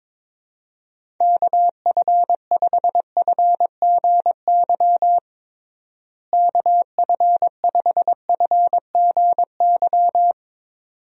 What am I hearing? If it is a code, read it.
Morse code, heard as a keyed tone, about 22 wpm, 700 Hz. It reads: KF5FGY KF5FGY
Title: USA Call Signs - 6 Characters - Repeated 2x - 22wpm